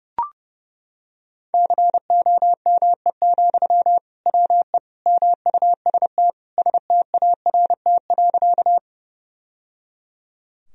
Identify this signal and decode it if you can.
Morse code, heard as a keyed tone, about 30 wpm, 700 Hz. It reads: COME, WE MUST START.
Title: Sentences from Top 1000 Words - Review Part 1 - 30wpm